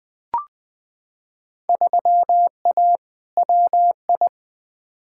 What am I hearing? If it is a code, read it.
Morse code, heard as a keyed tone, about 20 wpm, 700 Hz. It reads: 3A WI
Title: Field Day Exchanges - 20wpm